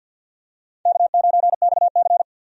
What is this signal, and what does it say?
Morse code, heard as a keyed tone, about 46 wpm, 700 Hz. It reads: K9XC
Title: USA Call Signs - 4 Characters - 50wpm